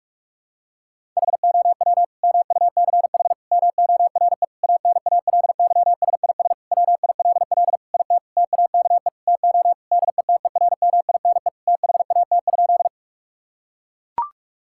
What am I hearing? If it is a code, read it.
Morse code, heard as a keyed tone, about 41 wpm, 700 Hz. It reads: HOW MUCH MORE ANALYSIS WILL IT TAKE TO DETERMINE THAT?
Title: Sentences from Top 1000 Words - Review Part 5 - 45wpm